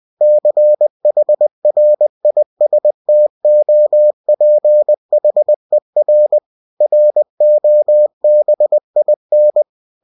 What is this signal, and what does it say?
Morse code, heard as a keyed tone, about 20 wpm, 600 Hz. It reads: CHRISTOPHER ROBIN